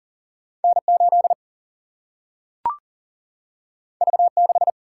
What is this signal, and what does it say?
Morse code, heard as a keyed tone, about 42 wpm, 700 Hz. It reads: N 8 V 6
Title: Single-Letter Single-Number 40wpm